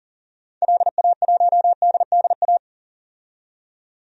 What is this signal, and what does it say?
Morse code, heard as a keyed tone, about 40 wpm, 700 Hz. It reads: LA1DDA
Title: DXCC-100 Call Signs - 6 Characters - 40wpm